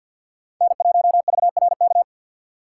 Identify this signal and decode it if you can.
Morse code, heard as a keyed tone, about 53 wpm, 700 Hz. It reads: N1VRK
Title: USA Call Signs - 6x Character Spacing - 50wpm